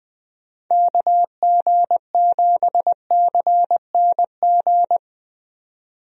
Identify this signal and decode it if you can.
Morse code, heard as a keyed tone, about 20 wpm, 700 Hz. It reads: KG7CNG
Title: USA Call Signs - 6 Characters - 20wpm